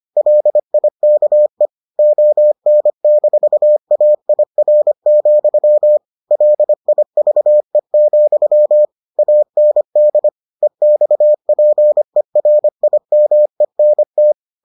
Morse code, heard as a keyed tone, about 25 wpm, 600 Hz. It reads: LIKE ON-AIR, LIVE, AND EXPERIMENT